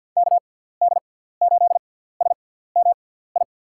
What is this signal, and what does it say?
Morse code, heard as a keyed tone, about 47 wpm, spaced as for 17 wpm, 700 Hz. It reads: KD8SMI
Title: USA Call Signs - 6 Characters - 6x Character Spacing - 50wpm